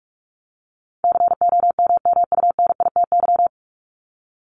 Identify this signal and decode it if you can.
Morse code, heard as a keyed tone, about 44 wpm, 700 Hz. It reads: COMMUNITY
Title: Top 501-750 Words 45wpm